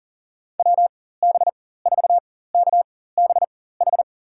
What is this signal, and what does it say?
Morse code, heard as a keyed tone, about 40 wpm, spaced as for 19 wpm, 700 Hz. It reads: WB4KBH
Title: USA Call Signs - 6 Characters - 4x Character Spacing - 40wpm